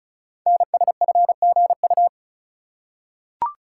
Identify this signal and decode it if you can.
Morse code, heard as a keyed tone, about 36 wpm, 700 Hz. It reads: NSFGU
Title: Code Groups - 5 Letters - 35wpm